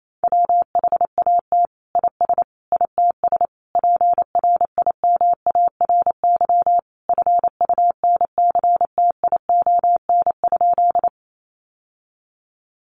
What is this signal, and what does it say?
Morse code, heard as a keyed tone, about 28 wpm, 700 Hz. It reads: WHAT IS ITS PRIMARY FUNCTION?
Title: Sentences from Top 1100 Words 28wpm